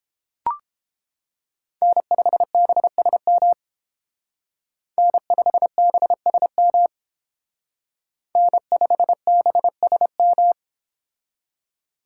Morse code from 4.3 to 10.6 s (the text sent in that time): N5BSM N5BSM